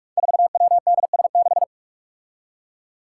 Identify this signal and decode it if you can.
Morse code, heard as a keyed tone, about 44 wpm, 700 Hz. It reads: 4WDS6
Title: Five Challenging Letter-Number Combinations 45wpm